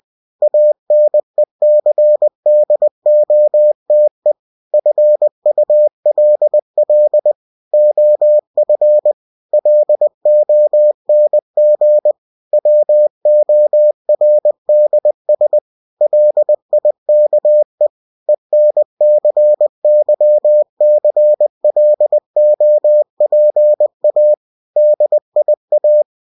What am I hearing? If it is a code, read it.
Morse code, heard as a keyed tone, about 20 wpm, 600 Hz. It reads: ANECDOTE FULL OF LONG WORDS LIKE ENCYCLOPA DIA
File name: cw-books / winnie-the-pooh / winnie-20wpm-page-0085